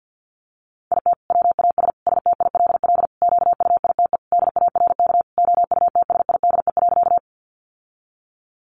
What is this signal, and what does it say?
Morse code, heard as a keyed tone, about 50 wpm, 700 Hz. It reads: IT WAS STILL QUITE DARK OUTSIDE.
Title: Sentences from Top 900 Words 50wpm